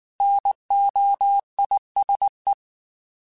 Morse code, heard as a keyed tone, about 19 wpm, 800 Hz. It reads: NOISE
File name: Alphabetical / Alphabetical mp3s / Ns @18